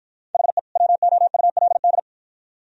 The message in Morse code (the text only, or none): HE WOULD